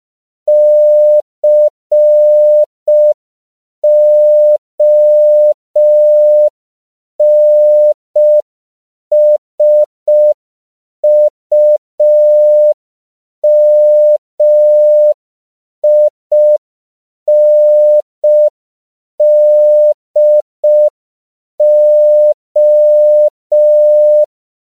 Morse code, heard as a keyed tone, about 5 wpm, 600 Hz. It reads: CONSUMINDO
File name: 5PPM